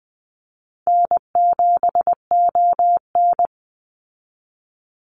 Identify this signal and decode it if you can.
Morse code, heard as a keyed tone, about 20 wpm, 700 Hz. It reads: N7ON
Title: Easy Call Signs - 20wpm